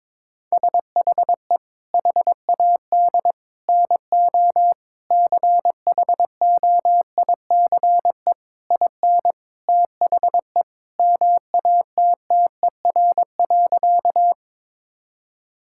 Morse code, heard as a keyed tone, about 21 wpm, 700 Hz. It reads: SHE HAD NO CHOICE IN THE MATTER.
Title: Sentences from Top 1000 Words - Review Part 6 - 22wpm